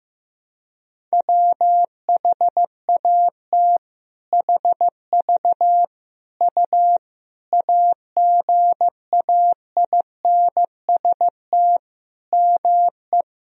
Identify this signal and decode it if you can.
Morse code, heard as a keyed tone, about 15 wpm, 700 Hz. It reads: WHAT HV U AGAINST ME
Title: Sentences from Top 400 Words and Abbreviations 15wpm